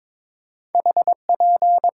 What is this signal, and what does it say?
Morse code, heard as a keyed tone, about 22 wpm, 700 Hz. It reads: HP
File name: Top 90 QSO Elements - 6x Character Spacing - 22wpm